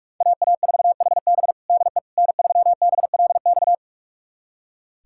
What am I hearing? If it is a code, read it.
Morse code, heard as a keyed tone, about 46 wpm, 700 Hz. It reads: AA4HB DE N3BLX
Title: USA Callsign DE Callsign - 5 Characters - 45wpm